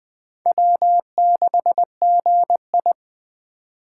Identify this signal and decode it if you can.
Morse code, heard as a keyed tone, about 20 wpm, 700 Hz. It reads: W6GI
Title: Speed Racing - USA 1x2 Call Signs 17wpm